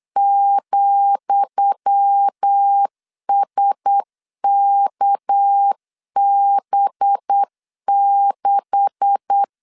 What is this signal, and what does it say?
Morse code, heard as a keyed tone, about 8 wpm, 800 Hz. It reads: ,SKB6
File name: LEZIONI / 10 SIGNS / 10D COMA 8wpm 3749